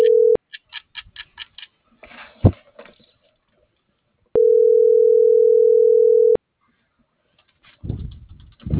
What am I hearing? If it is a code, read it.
Ringback tone (North America), 440 + 480 Hz.